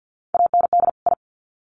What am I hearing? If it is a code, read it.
Morse code, heard as a keyed tone, about 53 wpm, 700 Hz. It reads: AND I
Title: Sets of 2 Words 50wpm